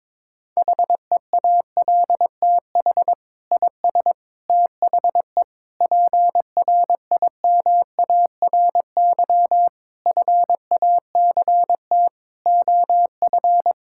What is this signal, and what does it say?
Morse code, heard as a keyed tone, about 22 wpm, 700 Hz. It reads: HEALTH IS THE PRIMARY FACT OF
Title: Sentences from Top 1000 Words - Review Part 5 - 22wpm